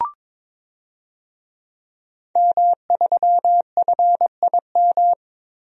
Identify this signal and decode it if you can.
Morse code, heard as a keyed tone, about 22 wpm, 700 Hz. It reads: M3FIM
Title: DXCC-100 Call Signs - 5 Characters - 22wpm